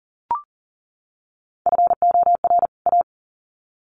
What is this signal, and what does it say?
Morse code, heard as a keyed tone, about 40 wpm, 700 Hz. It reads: FOR A
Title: Sets of 2 Words 40wpm